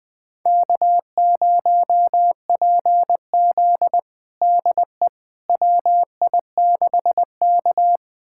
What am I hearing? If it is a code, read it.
Morse code, heard as a keyed tone, about 20 wpm, 700 Hz. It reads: K0PZ DE WI6K
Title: USA Callsign DE Callsign - 4 Characters - 20wpm